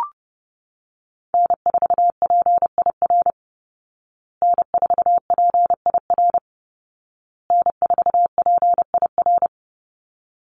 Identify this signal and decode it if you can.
Morse code, heard as a keyed tone, about 30 wpm, 700 Hz. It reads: N4PIR N4PIR N4PIR
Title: Call Signs Repeated 2x - 30wpm